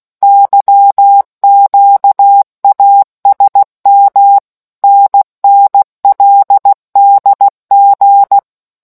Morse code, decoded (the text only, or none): YQASM NNLDG